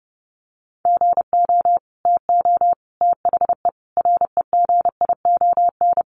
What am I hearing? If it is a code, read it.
Morse code, heard as a keyed tone, about 29 wpm, 700 Hz. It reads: GO TO THE REGION
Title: Sentences from Top 1500 Words 30wpm